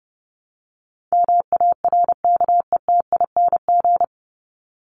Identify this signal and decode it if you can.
Morse code, heard as a keyed tone, about 30 wpm, 700 Hz. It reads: MARKETING
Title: Top 751-1000 Words 30wpm